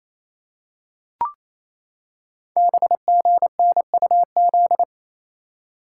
Morse code, heard as a keyed tone, about 27 wpm, 700 Hz. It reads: BGNUZ
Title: Code Groups - 5 Letters - 28wpm